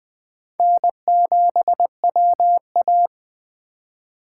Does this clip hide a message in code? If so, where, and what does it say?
Yes: Morse code from 0.6 to 3.1 s, heard as a keyed tone, about 20 wpm, 700 Hz. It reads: N7WA